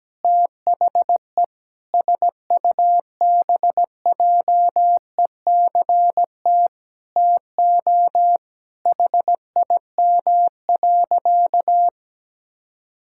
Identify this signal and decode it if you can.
Morse code, heard as a keyed tone, about 17 wpm, 700 Hz. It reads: THE SUBJECT TO HIM.
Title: Sentences from Top 1600 Words 17wpm